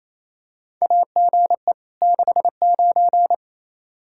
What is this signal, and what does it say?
Morse code, heard as a keyed tone, about 28 wpm, 700 Hz. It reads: AGE 69
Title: Practice QSO - Exchange 3 - More Info - 28wpm